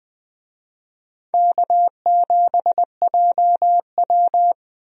Morse code, heard as a keyed tone, about 21 wpm, 700 Hz. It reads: K7JW
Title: USA Call Signs - 4 Characters - 20wpm